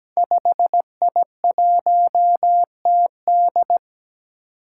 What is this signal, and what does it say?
Morse code, heard as a keyed tone, about 17 wpm, 700 Hz. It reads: 5I1TD
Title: Five Challenging Letter-Number Combinations 17wpm